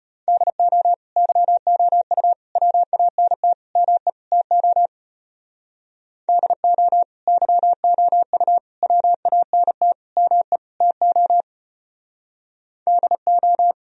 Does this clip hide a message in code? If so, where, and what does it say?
Yes: Morse code from 0.3 to 13.7 s, heard as a keyed tone, about 32 wpm, 700 Hz. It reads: DO YOU WANT ME TO DO YOU WANT ME TO DO